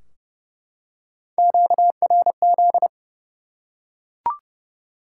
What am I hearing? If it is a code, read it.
Morse code, heard as a keyed tone, about 30 wpm, 700 Hz. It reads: QRZ
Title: Top 90 QSO Elements 30wpm